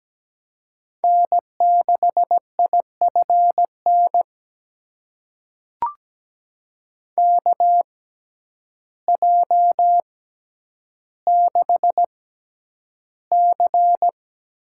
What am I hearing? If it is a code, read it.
Morse code, heard as a keyed tone, about 17 wpm, spaced as for 9 wpm, 700 Hz. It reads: N6IFN KJ6C
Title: USA Call Signs - 6x Character Spacing - 17wpm